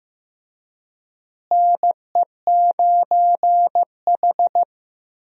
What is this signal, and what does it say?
Morse code, heard as a keyed tone, about 15 wpm, 700 Hz. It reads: NE9H
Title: USA Call Signs - 4 Characters - Repeated 3x - 15wpm